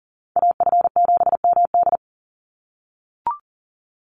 Morse code, heard as a keyed tone, about 40 wpm, 700 Hz. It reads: AF7MD
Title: NA SOTA Chaser Callsigns 40wpm